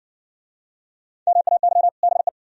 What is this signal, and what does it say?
Morse code, heard as a keyed tone, about 57 wpm, 700 Hz. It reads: MAY BE